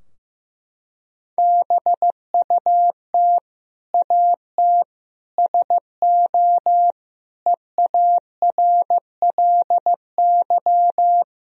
Morse code, heard as a keyed tone, about 15 wpm, 700 Hz. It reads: BUT AT SO EARLY
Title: Sentences from Top 1000 Words - Review Part 3 - 15wpm